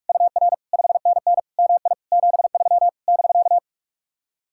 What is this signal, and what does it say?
Morse code, heard as a keyed tone, about 58 wpm, 700 Hz. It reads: UR 5NN MI 73 <BK>